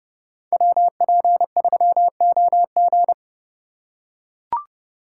Morse code, heard as a keyed tone, about 30 wpm, 700 Hz. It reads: WP3OG
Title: USA Call Signs - 5 Characters - 3x Character Spacing - 30wpm